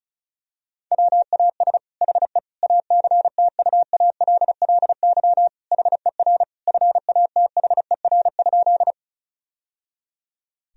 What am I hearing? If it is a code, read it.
Morse code, heard as a keyed tone, about 34 wpm, 700 Hz. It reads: WAS HE ACTUALLY HER FATHER?
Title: Sentences from Top 1000 Words - Review Part 4 - 35wpm